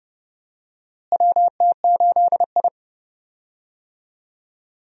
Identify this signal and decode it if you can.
Morse code, heard as a keyed tone, about 30 wpm, 700 Hz. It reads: WT8I
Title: USA Call Signs - 4 Characters - Repeated 2x - 30wpm